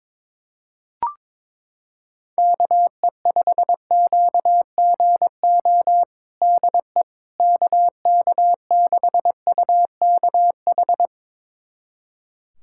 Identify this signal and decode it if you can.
Morse code, heard as a keyed tone, about 22 wpm, 700 Hz. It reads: KE5QGO DE KK6UKH
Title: USA Callsign DE Callsign - 6 Characters - 22wpm